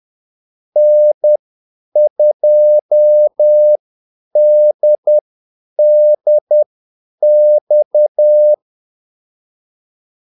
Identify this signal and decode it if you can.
Morse code, heard as a keyed tone, about 10 wpm, spaced as for 8 wpm, 600 Hz. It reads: N2DDX
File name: Callsigns-10-8wpm